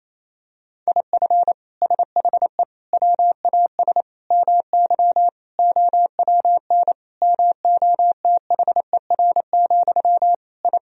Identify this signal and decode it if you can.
Morse code, heard as a keyed tone, about 28 wpm, 700 Hz. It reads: IF SHE WAS MY OWN MOTHER, I
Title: Sentences from Top 1000 Words - Review Part 2 - 28wpm